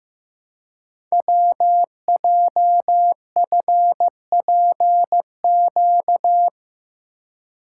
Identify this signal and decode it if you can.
Morse code, heard as a keyed tone, about 15 wpm, 700 Hz. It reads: WJFPQ
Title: Code Groups - 5 Letters - 15wpm